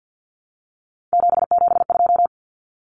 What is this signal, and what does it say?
Morse code, heard as a keyed tone, about 50 wpm, 700 Hz. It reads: 772
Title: Numbers - 3-Digit 50wpm